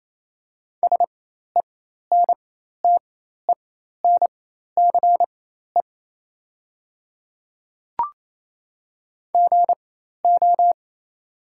Morse code, heard as a keyed tone, about 28 wpm, spaced as for 13 wpm, 690 Hz. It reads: SENTENCE GO
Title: 4x Character Spacing - Top 200 Words - 28wpm